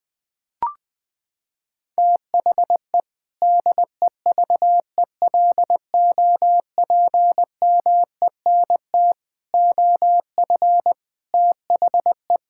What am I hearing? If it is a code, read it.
Morse code, heard as a keyed tone, about 20 wpm, 700 Hz. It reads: THE DEVELOPMENT OF THE
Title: Sentences with 3-Word Phrases - 20wpm